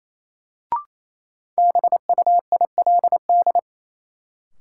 Morse code, heard as a keyed tone, about 28 wpm, 700 Hz. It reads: BUILD
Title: Sentences from Top 1000 Words - Review Part 1 - New Word Drill - 28wpm